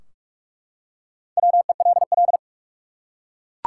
Morse code, heard as a keyed tone, about 44 wpm, 700 Hz. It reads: WELL
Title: Top 180 QSO Elements 45wpm